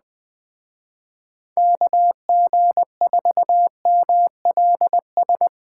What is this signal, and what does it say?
Morse code, heard as a keyed tone, about 20 wpm, 700 Hz. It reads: KG4MLS
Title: Speed Racing - USA Call Signs 17wpm